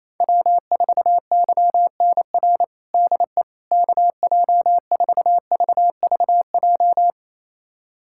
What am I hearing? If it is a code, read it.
Morse code, heard as a keyed tone, about 28 wpm, 700 Hz. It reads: W4YNR DE KJ4VVJ
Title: USA Callsign DE Callsign 28wpm